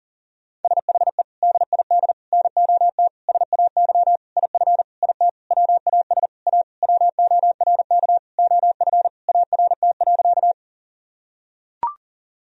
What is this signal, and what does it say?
Morse code, heard as a keyed tone, about 40 wpm, 700 Hz. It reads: SHE DID NOT SAY IF IT WAS A WORK OF ART.
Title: Sentences from Top 1000 Words - Review Part 3 - 40wpm